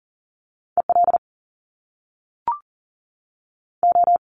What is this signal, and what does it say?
Morse code, heard as a keyed tone, about 39 wpm, 700 Hz. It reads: EL O
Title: Top Two Letter Combinations in Words 40wpm